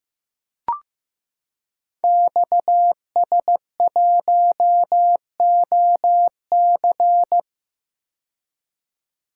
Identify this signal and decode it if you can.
Morse code, heard as a keyed tone, about 15 wpm, 700 Hz. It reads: XS1OC
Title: Code Groups - 5 Letter:Numbers - 15wpm